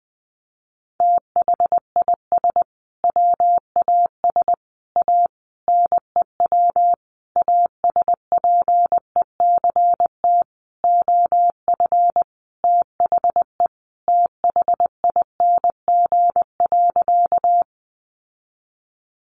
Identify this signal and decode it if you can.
Morse code, heard as a keyed tone, about 20 wpm, 700 Hz. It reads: THIS WAS A NEW ASPECT OF THE THING.
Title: Sentences from Top 1300 Words 20wpm